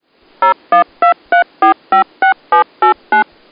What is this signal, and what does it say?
DTMF, heard as telephone touch tones, tones of about 120 ms. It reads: 0123456789
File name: dtmf tones fm 1c